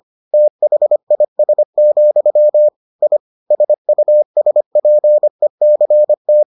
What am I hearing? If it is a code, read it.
Morse code, heard as a keyed tone, about 25 wpm, 600 Hz. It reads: THIS, I SUSPECT